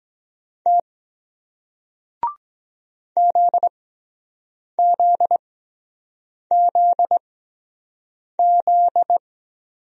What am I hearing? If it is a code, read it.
Morse code, heard as a keyed tone, about 23 wpm, 700 Hz. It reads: T Z Z Z Z